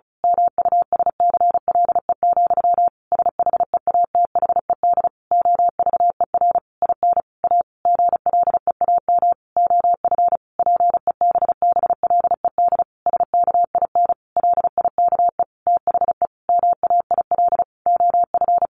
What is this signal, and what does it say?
Morse code, heard as a keyed tone, about 34 wpm, 700 Hz. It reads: MUSCLE, SHEATHED OVER IN A GLEAM OF PEBBLED SKIN LIKE THE MAIL OF
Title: A Sound Of Thunder 35wpm